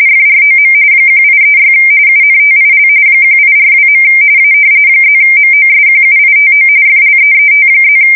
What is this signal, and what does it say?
RTTY, about 45.45 baud. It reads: FROM 1836 UNTIL 1842. A +MOSQUITO FLEET+
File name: fsk170 / usnavy2